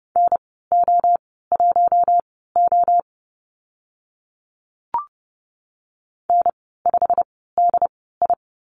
Morse code, heard as a keyed tone, about 30 wpm, spaced as for 17 wpm, 690 Hz. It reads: NO1O N5DI
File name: Lesson 15g - Number 5 - Call Signs 30wpm